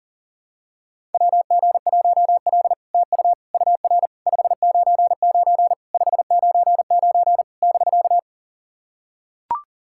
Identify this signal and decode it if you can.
Morse code, heard as a keyed tone, about 44 wpm, 700 Hz. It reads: WG1L TU UR 599 599 <BK>
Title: SOTA QSO Practice 40wpm